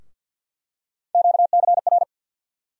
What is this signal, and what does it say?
Morse code, heard as a keyed tone, about 51 wpm, 700 Hz. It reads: QCR
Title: USA Call Sign Postfixes - 3 Letters 50wpm